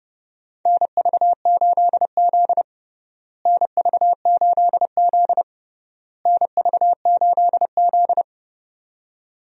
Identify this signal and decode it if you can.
Morse code, heard as a keyed tone, about 30 wpm, 700 Hz. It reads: NV8Z NV8Z NV8Z
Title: USA Call Signs - 4 Characters - Repeated 3x - 30wpm